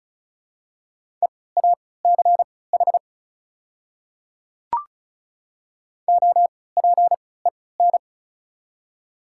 Morse code, heard as a keyed tone, about 34 wpm, spaced as for 20 wpm, 700 Hz. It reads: EACH OPEN
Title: 3x Character Spacing - Top 300 Words - 35wpm